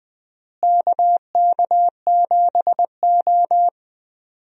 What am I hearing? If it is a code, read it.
Morse code, heard as a keyed tone, about 20 wpm, 700 Hz. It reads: KK7O
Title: USA Call Signs - 4 Characters - Repeated 3x - 20wpm